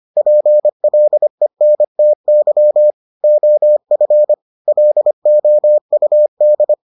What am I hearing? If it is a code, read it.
Morse code, heard as a keyed tone, about 25 wpm, 600 Hz. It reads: PLENTY OF LOUD